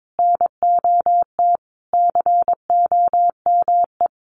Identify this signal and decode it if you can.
Morse code, heard as a keyed tone, about 22 wpm, 700 Hz. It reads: NOT COME
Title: Sentences from Top 1500 Words 22wpm